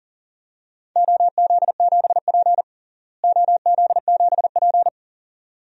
Morse code, heard as a keyed tone, about 40 wpm, 700 Hz. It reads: OZ7P OZ7P
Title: DXCC-100 Call Signs - 4 Characters - Repeated 2x - 40wpm